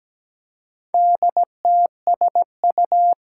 Morse code, heard as a keyed tone, about 17 wpm, 700 Hz. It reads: DTSU
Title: Four Challenging Letter-Number Combinations 17wpm